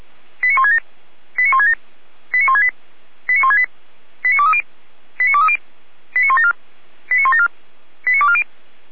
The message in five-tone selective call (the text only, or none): 89167 89167 89168 89168 89230 89230 89174 89174 89240